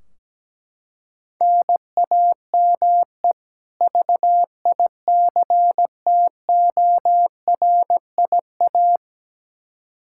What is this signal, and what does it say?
Morse code, heard as a keyed tone, about 17 wpm, 700 Hz. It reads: NAME VICTORIA
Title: Practice QSO - Exchange 1 - Beginning - 17wpm